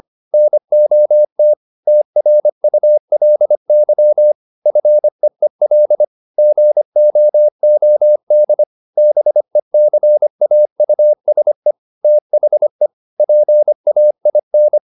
Morse code, heard as a keyed tone, about 25 wpm, 600 Hz. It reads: NOT TRULY FEEL GOOD BECAUSE THE PAIN